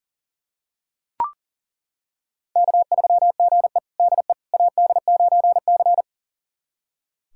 Morse code, heard as a keyed tone, about 40 wpm, 700 Hz. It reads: K3GE DE AD9C
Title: USA Callsign DE Callsign - 4 Characters - 40wpm